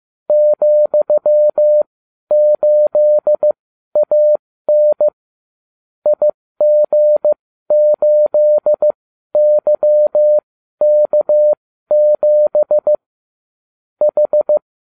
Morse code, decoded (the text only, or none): ,8AN IG8YK7 H